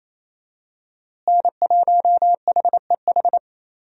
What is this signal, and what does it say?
Morse code, heard as a keyed tone, about 28 wpm, 700 Hz. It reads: N1HEH
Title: USA Call Signs - 5 Characters - Repeated 2x - 28wpm